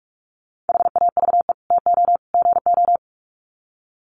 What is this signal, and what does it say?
Morse code, heard as a keyed tone, about 44 wpm, 700 Hz. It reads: HAVE TO GO
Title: Sets of 3 Words 45wpm